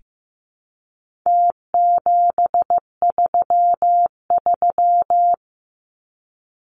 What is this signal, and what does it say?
Morse code, heard as a keyed tone, about 15 wpm, 700 Hz. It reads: T733
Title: Numbers - Contest Style - 4-Digit 15wpm